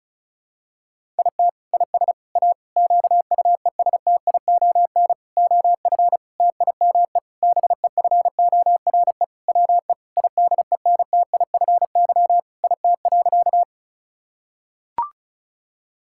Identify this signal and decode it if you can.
Morse code, heard as a keyed tone, about 36 wpm, 700 Hz. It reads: IT IS A QUESTION OF TIME BEFORE WE IDENTIFY IT.
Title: Sentences from Top 1000 Words - Review Part 6 - 35wpm